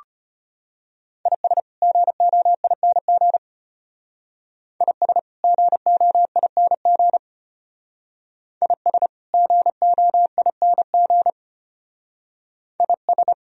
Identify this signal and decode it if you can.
Morse code, heard as a keyed tone, about 30 wpm, 700 Hz. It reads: IS GOING IS GOING IS GOING IS
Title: Speed-Racing - Sets of 2 Words 25wpm